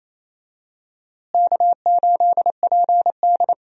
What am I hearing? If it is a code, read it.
Morse code, heard as a keyed tone, about 28 wpm, 700 Hz. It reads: K8PD
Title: USA Call Signs - 4 Characters - Repeated 3x - 28wpm